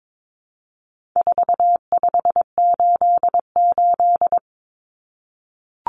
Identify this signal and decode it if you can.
Morse code, heard as a keyed tone, about 22 wpm, 700 Hz. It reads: 4588
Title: Common Cut Numbers - 4-Digit 22wpm